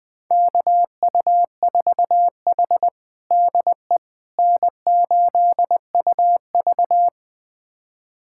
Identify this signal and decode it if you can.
Morse code, heard as a keyed tone, about 20 wpm, 700 Hz. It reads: KU4H DE N8UV
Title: USA Callsign DE Callsign - 4 Characters - 20wpm